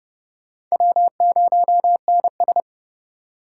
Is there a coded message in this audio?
Yes: Morse code from 0.7 to 2.6 s, heard as a keyed tone, about 40 wpm, 700 Hz. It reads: W0NS